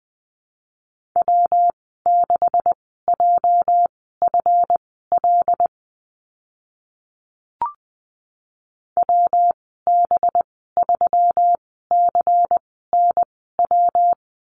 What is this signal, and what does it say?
Morse code, heard as a keyed tone, about 20 wpm, spaced as for 14 wpm, 700 Hz. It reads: W6JFL WB3CNW
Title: 231 Lesson 37g - Letter x - Call Signs 20wpm